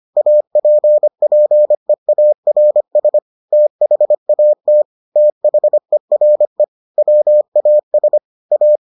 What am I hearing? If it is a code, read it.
Morse code, heard as a keyed tone, about 25 wpm, 600 Hz. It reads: APPEARS THAT THERE WAS A